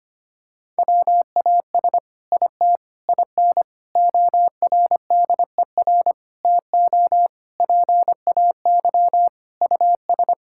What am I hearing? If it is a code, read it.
Morse code, heard as a keyed tone, about 25 wpm, 700 Hz. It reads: WAS IT IN ORDER TO PAY US
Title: Sentences from Top 1000 Words - Review Part 6 - 25wpm